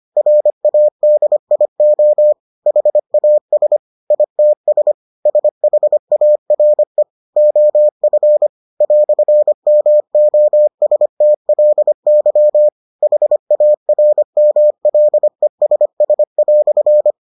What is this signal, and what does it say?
Morse code, heard as a keyed tone, about 25 wpm, 600 Hz. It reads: RADIO HAS ITS SHARE OF "MOSTLY HARMLESS"